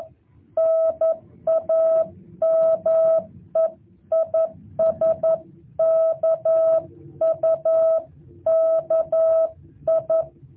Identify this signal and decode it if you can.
Morse code, heard as a keyed tone, about 11 wpm, 650 Hz. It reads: NAME IS KUKI
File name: jh8cbh / 2023cw